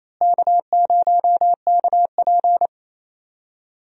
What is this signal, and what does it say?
Morse code, heard as a keyed tone, about 27 wpm, 700 Hz. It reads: K0KP
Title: USA Call Signs - 4 Characters - Repeated 3x - 28wpm